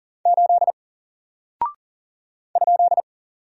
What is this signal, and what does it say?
Morse code, heard as a keyed tone, about 39 wpm, 700 Hz. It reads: 8 ?
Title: Single Character (Letter:Number:Punctuation:Prosign) 40wpm